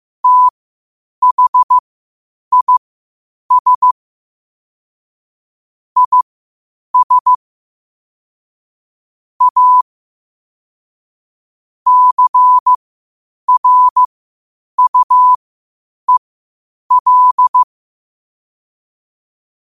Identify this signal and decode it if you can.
Morse code, heard as a keyed tone, about 15 wpm, spaced as for 8 wpm, 1000 Hz. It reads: THIS IS A CRUEL